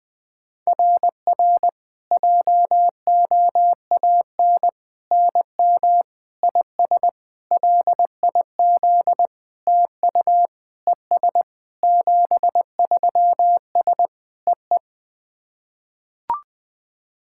Morse code, heard as a keyed tone, about 20 wpm, 690 Hz. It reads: RR JOAN NM IS LIZ TU ES 73S EE
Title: SOTA QSO Practice 20wpm